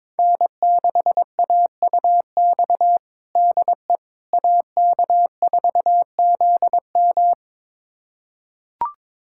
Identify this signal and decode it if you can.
Morse code, heard as a keyed tone, about 22 wpm, 700 Hz. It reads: N6AUX DE AK4ZM
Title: USA Callsign DE Callsign - 5 Characters - 22wpm